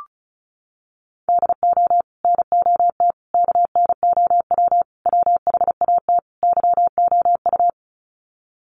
Morse code, heard as a keyed tone, about 34 wpm, 700 Hz. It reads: DO NOT KNOW WHAT YOU
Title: Sets of 5 Words 35wpm